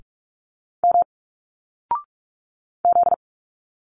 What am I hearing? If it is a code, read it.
Morse code, heard as a keyed tone, about 46 wpm, 700 Hz. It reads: M Z
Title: Single Letters 45wpm